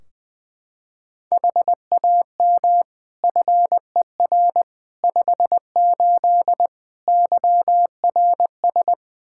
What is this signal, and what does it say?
Morse code, heard as a keyed tone, about 21 wpm, 700 Hz. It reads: HAM FER 58 YRS
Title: Practice QSO - Exchange 3 - More Info - 20wpm